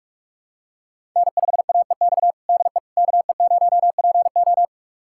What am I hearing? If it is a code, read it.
Morse code, heard as a keyed tone, about 46 wpm, 700 Hz. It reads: N5AEX DE KE0PO
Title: USA Callsign DE Callsign - 5 Characters - 45wpm